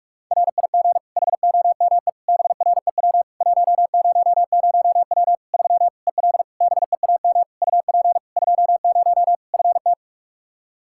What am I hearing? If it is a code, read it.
Morse code, heard as a keyed tone, about 46 wpm, 700 Hz. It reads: RIG HOME BREW 100W 3 EL BEAM UP 20 FT
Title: Practice QSO - Exchange 2 - Info - 45wpm